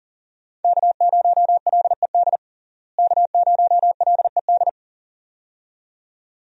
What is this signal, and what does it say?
Morse code, heard as a keyed tone, about 40 wpm, 700 Hz. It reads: K0LED K0LED
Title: USA Call Signs - 5 Characters - Repeated 2x - 40wpm